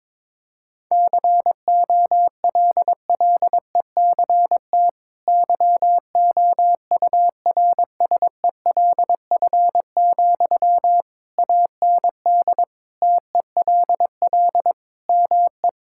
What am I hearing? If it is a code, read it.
Morse code, heard as a keyed tone, about 21 wpm, 700 Hz. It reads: COLLECT YOURSELF, AND TELL ME
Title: Sentences from Top 1600 Words 22wpm